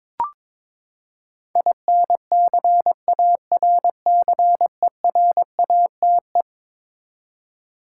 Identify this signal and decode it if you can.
Morse code, heard as a keyed tone, about 22 wpm, 700 Hz. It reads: INCARCERATE
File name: practice / 22wpm / Eleven Letter Words - 22wpm